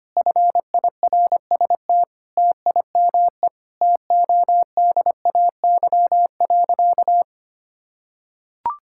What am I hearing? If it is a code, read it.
Morse code, heard as a keyed tone, about 25 wpm, 700 Hz. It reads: FIRST TIME TODAY.
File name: practice / 25wpm / Sentences with 4-Word Phrases - 25wpm